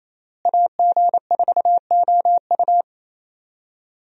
Morse code, heard as a keyed tone, about 28 wpm, 700 Hz. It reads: AG4OU
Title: USA Call Signs - 5 Characters - Repeated 3x - 28wpm